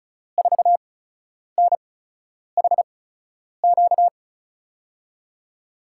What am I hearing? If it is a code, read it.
Morse code, heard as a keyed tone, about 36 wpm, spaced as for 10 wpm, 700 Hz. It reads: 4NHQ
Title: USA Call Signs - 5 Characters - 8x Character Spacing - 35wpm